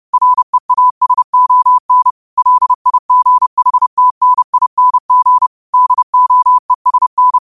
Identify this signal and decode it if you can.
Morse code, heard as a keyed tone, about 30 wpm, 1000 Hz. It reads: REASON LIGHTNING DOESN
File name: qotdfiles / 30wpm